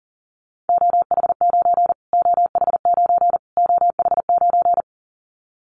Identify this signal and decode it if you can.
Morse code, heard as a keyed tone, about 40 wpm, 700 Hz. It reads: OH9 OH9 OH9
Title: Callsign Prefixes for Top 25 DXCC - Repeated 3x - 40wpm